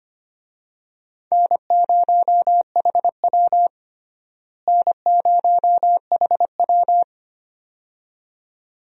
Morse code, heard as a keyed tone, about 25 wpm, 700 Hz. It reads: N0HW N0HW
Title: USA Call Signs - 4 Characters - Repeated 2x - 25wpm